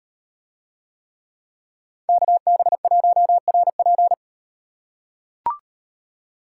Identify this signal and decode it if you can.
Morse code, heard as a keyed tone, about 38 wpm, 700 Hz. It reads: KB1RP